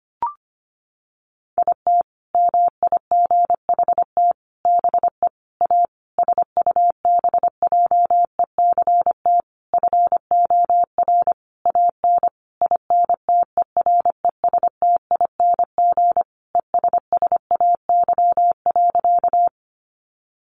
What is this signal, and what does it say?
Morse code, heard as a keyed tone, about 25 wpm, 700 Hz. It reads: IT MIGHT BE A SUBJECT FOR AN INTERESTING ESSAY.
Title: Sentences from Top 1500 Words 25wpm